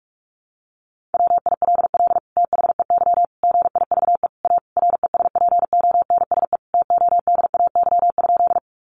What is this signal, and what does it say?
Morse code, heard as a keyed tone, about 44 wpm, 700 Hz. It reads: WILL THEY GIVE A RESPONSE TODAY?
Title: Sentences from Top 1400 Words 45wpm